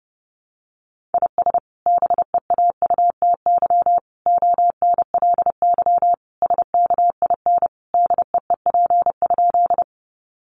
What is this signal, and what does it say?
Morse code, heard as a keyed tone, about 30 wpm, 700 Hz. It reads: IS BEAUTY ONLY SKIN DEEP?
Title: Sentences from Top 1300 Words 30wpm